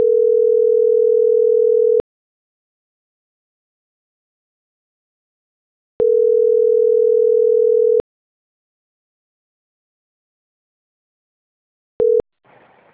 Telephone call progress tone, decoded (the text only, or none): ringback tone